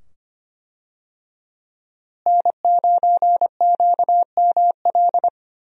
Morse code, heard as a keyed tone, about 25 wpm, 700 Hz. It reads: N9QML
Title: USA Call Signs - 5 Characters - Repeated 2x - 25wpm